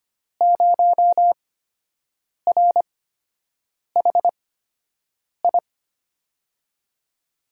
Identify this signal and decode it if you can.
Morse code, heard as a keyed tone, about 25 wpm, spaced as for 7 wpm, 700 Hz. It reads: 0RHI
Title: USA Call Signs - 5 Characters - 8x Character Spacing - 25wpm